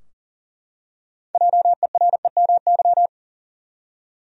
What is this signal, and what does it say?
Morse code, heard as a keyed tone, about 40 wpm, 700 Hz. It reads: JEREMY
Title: USA Common Names - 40wpm